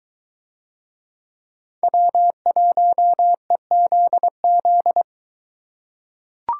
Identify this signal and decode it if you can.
Morse code, heard as a keyed tone, about 22 wpm, 700 Hz. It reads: W1EZZ